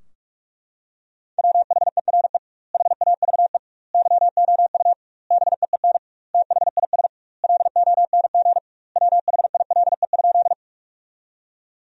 Morse code, heard as a keyed tone, about 42 wpm, 700 Hz. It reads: WHERE HAVE YOU BEEN THIS LONG WHILE?
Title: 2x Word-Spacing - Sentences from Top 300 Words - 45wpm